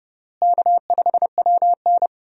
Morse code, heard as a keyed tone, about 30 wpm, 700 Hz. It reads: K5WN
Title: USA Call Signs - 4 Characters - Repeated 3x - 30wpm